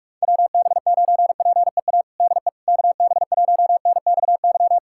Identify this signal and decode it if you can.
Morse code, heard as a keyed tone, about 42 wpm, 700 Hz. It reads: WB9PEA DE KB1NXY
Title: USA Callsign DE Callsign - 6 Characters - 45wpm